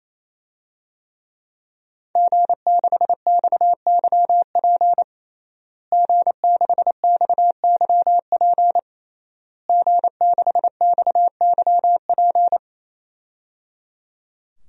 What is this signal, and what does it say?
Morse code, heard as a keyed tone, about 27 wpm, 700 Hz. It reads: G6XYP G6XYP G6XYP
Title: DXCC-100 Call Signs - 5 Characters - Repeated 3x - 28wpm